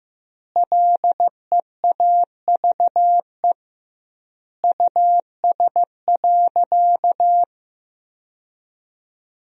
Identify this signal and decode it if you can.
Morse code, heard as a keyed tone, about 15 wpm, 700 Hz. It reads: LEAVE US.
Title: 2x Word-Spacing - Sentences from Top 400 Words - 15wpm